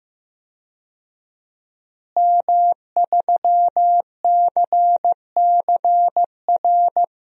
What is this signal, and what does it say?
Morse code, heard as a keyed tone, about 15 wpm, 700 Hz. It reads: M3CCR
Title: DXCC-100 Call Signs - 5 Characters - Repeated 2x - 15wpm